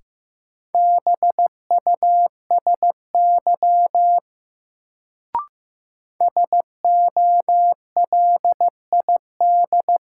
Morse code, heard as a keyed tone, about 15 wpm, 700 Hz. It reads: BUSY SOLID
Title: Sentences from Top 1500 Words - New Word Drill Alt - 15wpm